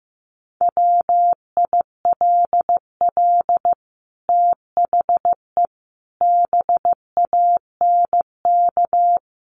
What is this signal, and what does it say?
Morse code, heard as a keyed tone, about 15 wpm, 700 Hz. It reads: WILL THE BANK
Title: Sentences from Top 1500 Words 15wpm